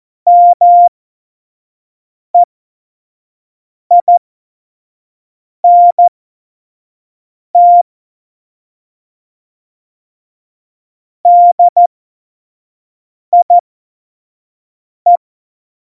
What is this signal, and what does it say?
Morse code, heard as a keyed tone, about 14 wpm, spaced as for 5 wpm, 700 Hz. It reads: MEINT DIE
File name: Morsekurs / Witze / Witz 11. Juni 2019, 25 F70